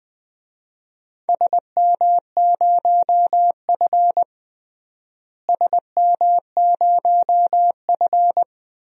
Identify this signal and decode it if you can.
Morse code, heard as a keyed tone, about 20 wpm, 700 Hz. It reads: SM0F SM0F
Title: DXCC-100 Call Signs - 4 Characters - Repeated 2x - 20wpm